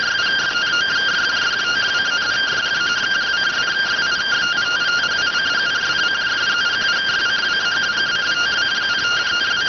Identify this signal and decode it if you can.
Two-tone teleprinter signal, about 170 Hz shift.